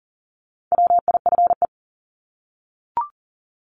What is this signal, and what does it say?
Morse code, heard as a keyed tone, about 40 wpm, 700 Hz. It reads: WIFE
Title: Top 1751-2000 Words 40wpm